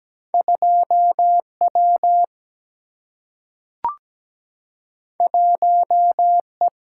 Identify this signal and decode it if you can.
Morse code, heard as a keyed tone, about 17 wpm, 700 Hz. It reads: 2W 1E
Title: Single-Letter Single-Number 17wpm